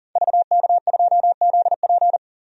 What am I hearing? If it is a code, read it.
Morse code, heard as a keyed tone, about 40 wpm, 700 Hz. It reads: VK2ZP
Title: DXCC-100 Call Signs - 5 Characters - Repeated 2x - 40wpm